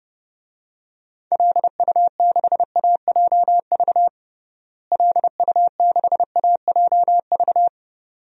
Morse code, heard as a keyed tone, about 30 wpm, 700 Hz. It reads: LU6AJV LU6AJV
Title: DXCC-100 Call Signs - 6 Characters - Repeated 2x - 30wpm